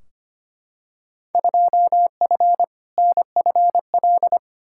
Morse code, heard as a keyed tone, about 25 wpm, 700 Hz. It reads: 2F NFL
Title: Field Day Exchanges - 25wpm